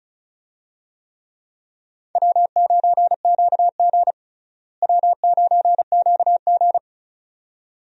Morse code, heard as a keyed tone, about 36 wpm, 700 Hz. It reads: W9QG W9QG
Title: USA Call Signs - 4 Characters - Repeated 2x - 35wpm